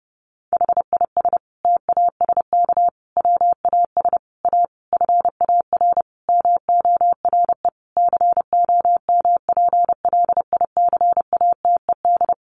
Morse code, heard as a keyed tone, about 30 wpm, 700 Hz. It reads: HIS TASK WAS A FAR MORE COMPLICATED
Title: Sentences from Top 1200 Words 30wpm